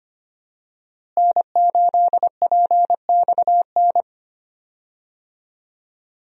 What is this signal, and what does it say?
Morse code, heard as a keyed tone, about 25 wpm, 700 Hz. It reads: N8PXN
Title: USA Call Signs - 5 Characters - Repeated 3x - 25wpm